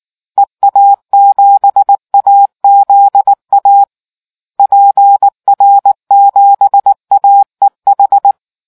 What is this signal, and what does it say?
Morse code, heard as a keyed tone, about 19 wpm, 800 Hz. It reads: EA7AZA PR7AEH